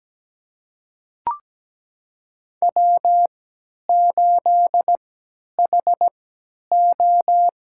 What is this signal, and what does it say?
Morse code, heard as a keyed tone, about 17 wpm, spaced as for 10 wpm, 700 Hz. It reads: W8HO
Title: USA Call Signs - 3x Character Spacing - 17wpm